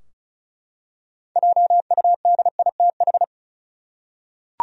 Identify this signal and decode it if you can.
Morse code, heard as a keyed tone, about 36 wpm, 700 Hz. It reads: JUDITH
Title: USA Common Names - 35wpm